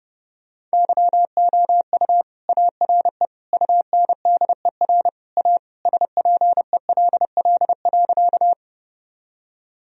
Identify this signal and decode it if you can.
Morse code, heard as a keyed tone, about 30 wpm, 700 Hz. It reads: YOU ARE UNDER A SPELL.
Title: Sentences from Top 1000 Words - Review Part 1 - 30wpm